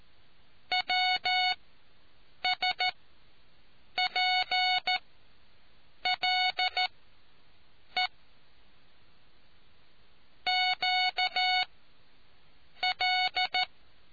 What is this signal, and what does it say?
Morse code, heard as a keyed tone, about 14 wpm, spaced as for 7 wpm, 740 Hz. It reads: WS P L E Q L